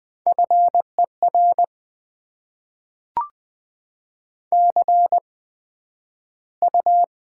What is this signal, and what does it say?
Morse code, heard as a keyed tone, about 20 wpm, spaced as for 11 wpm, 700 Hz. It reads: FER CU
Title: Top 20 QSO Elements - 8x Character Spacing - 20wpm